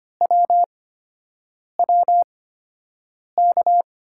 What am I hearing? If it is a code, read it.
Morse code, heard as a keyed tone, about 25 wpm, spaced as for 7 wpm, 700 Hz. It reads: WWK
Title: USA Call Signs - 8x Character Spacing - 25wpm